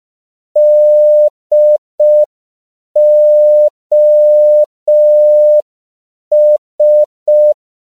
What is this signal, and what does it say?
Morse code, heard as a keyed tone, about 5 wpm, 600 Hz. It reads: DOS
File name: CW / 5PPM